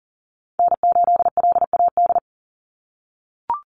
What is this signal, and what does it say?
Morse code, heard as a keyed tone, about 40 wpm, 700 Hz. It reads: N8LAD
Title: USA Call Signs 40wpm